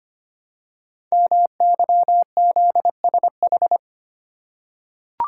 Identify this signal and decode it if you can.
Morse code, heard as a keyed tone, about 25 wpm, 700 Hz. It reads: MYZSH
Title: Code Groups - 5 Letter:Numbers - 25wpm